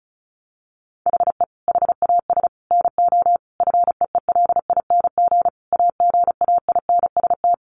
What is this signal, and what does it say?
Morse code, heard as a keyed tone, about 34 wpm, 700 Hz. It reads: HE HAS NO FEELING AGAINST
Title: Sentences from Top 1200 Words 35wpm